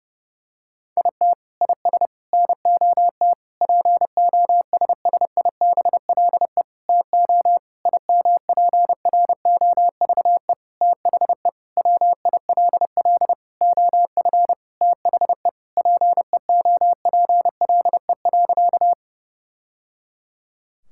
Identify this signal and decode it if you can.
Morse code, heard as a keyed tone, about 32 wpm, 700 Hz. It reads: IT IS NOT POSSIBLE TO IMPROVE THE WILL OF THE PEOPLE.
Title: Sentences from Top 1000 Words - Review Part 5 - 30wpm